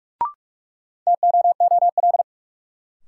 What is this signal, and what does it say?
Morse code, heard as a keyed tone, about 46 wpm, 700 Hz. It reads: TOOL
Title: Sentences from Top 1000 Words - Review Part 6 - New Word Drill - 45wpm